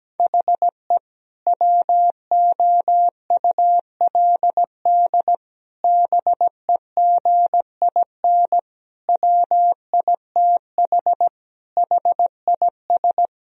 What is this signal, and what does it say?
Morse code, heard as a keyed tone, about 17 wpm, 700 Hz. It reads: HE WOULD BEGIN WITH HIS
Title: Sentences from Top 1000 Words - Review Part 1 - 17wpm